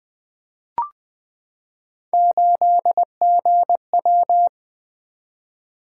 Morse code, heard as a keyed tone, about 20 wpm, 700 Hz. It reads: 8GW
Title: Three Letter-Number Combinations - 20wpm